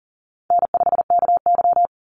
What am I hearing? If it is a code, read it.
Morse code, heard as a keyed tone, about 40 wpm, 700 Hz. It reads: N5KY
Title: Easy Call Signs Repeated 3x - 40wpm